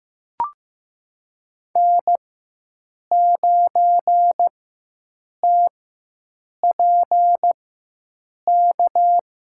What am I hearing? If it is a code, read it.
Morse code, heard as a keyed tone, about 16 wpm, spaced as for 7 wpm, 700 Hz. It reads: N9TPK
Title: USA Call Signs - 5 Characters - 4x Character Spacing - 15wpm